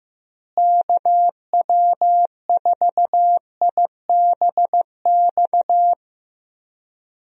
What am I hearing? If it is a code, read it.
Morse code, heard as a keyed tone, about 15 wpm, 700 Hz. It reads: KW4IBX
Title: USA Call Signs - 6 Characters - Repeated 3x - 15wpm